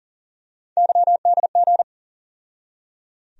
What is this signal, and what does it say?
Morse code, heard as a keyed tone, about 38 wpm, 700 Hz. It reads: YDG
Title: USA Call Sign Postfixes - 3 Letters 40wpm